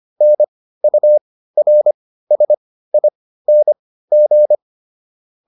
Morse code, heard as a keyed tone, about 25 wpm, spaced as for 15 wpm, 600 Hz. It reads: NURSING